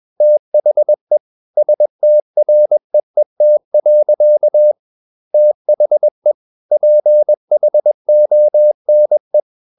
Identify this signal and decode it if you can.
Morse code, heard as a keyed tone, about 21 wpm, 600 Hz. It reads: THE STREET. THE PHONE
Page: CW med Gnister